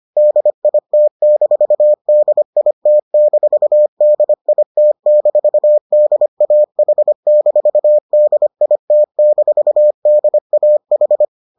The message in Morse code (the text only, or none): DIT-DIT-DIT-DAH-DIT-DAH